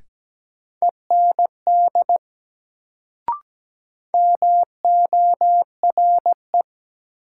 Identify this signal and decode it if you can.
Morse code, heard as a keyed tone, about 17 wpm, 700 Hz. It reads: END MORE
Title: Sets of 2 Words - New Word Drill - 17wpm